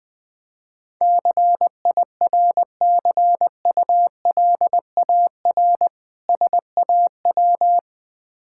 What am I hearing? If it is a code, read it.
Morse code, heard as a keyed tone, about 20 wpm, 700 Hz. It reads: CIRCULAR SAW
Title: Difficult Words with Hints - 20wpm